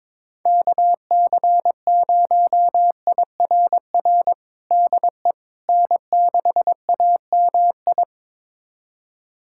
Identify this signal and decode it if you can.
Morse code, heard as a keyed tone, about 22 wpm, 700 Hz. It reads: KC0IRR DE N6AMI
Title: USA Callsign DE Callsign 22wpm